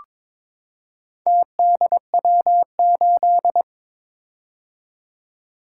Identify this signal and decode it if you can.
Morse code, heard as a keyed tone, about 22 wpm, 700 Hz. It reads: TDW8
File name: practice / 22wpm / Four Letter-Number Combinations - 22wpm